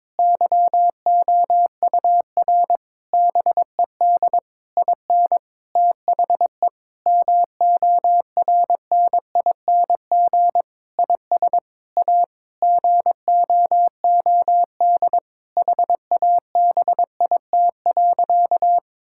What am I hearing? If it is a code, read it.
Morse code, heard as a keyed tone, about 22 wpm, 700 Hz. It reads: YOUR BED IN THE MORNING IS A GOOD HABIT.
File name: Difficult Sentences with Hints - 22wpm